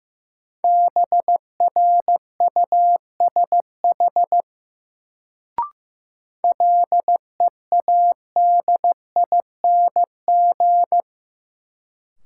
Morse code, heard as a keyed tone, about 15 wpm, 700 Hz. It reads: BRUSH LEADING
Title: Sentences from Top 1100 Words - New Word Drill Alt - 15wpm